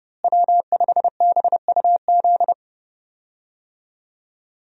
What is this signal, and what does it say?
Morse code, heard as a keyed tone, about 31 wpm, 700 Hz. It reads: W5BUZ
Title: USA Call Signs - 5 Characters - Repeated 2x - 30wpm